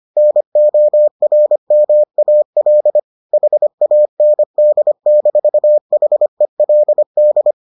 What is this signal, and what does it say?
Morse code, heard as a keyed tone, about 25 wpm, 600 Hz. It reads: NORMAL HAND-HELD